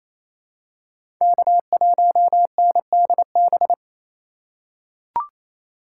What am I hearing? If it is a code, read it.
Morse code, heard as a keyed tone, about 27 wpm, 700 Hz. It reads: K1NDB
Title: USA Call Signs - 5 Characters - 2x Character Spacing - 28wpm